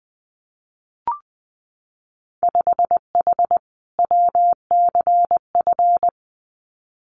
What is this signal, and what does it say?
Morse code, heard as a keyed tone, about 20 wpm, 700 Hz. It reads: 5H WCF
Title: Winter Field Day Exchanges Encore 20wpm